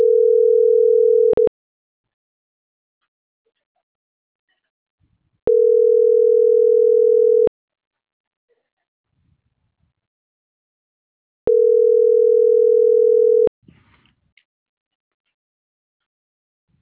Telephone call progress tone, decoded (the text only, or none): ringback tone